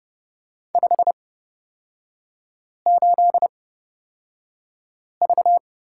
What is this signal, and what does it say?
Morse code, heard as a keyed tone, about 30 wpm, spaced as for 5 wpm, 700 Hz. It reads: 58V
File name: Lesson 37e - Letter x - Three Characters 30wpm